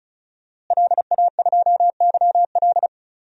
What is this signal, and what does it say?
Morse code, heard as a keyed tone, about 36 wpm, 700 Hz. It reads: LA2YL
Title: DXCC-100 Call Signs - 5 Characters - Repeated 2x - 35wpm